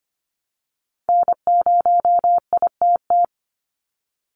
Morse code, heard as a keyed tone, about 25 wpm, 700 Hz. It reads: N0ITT
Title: USA Call Signs 25wpm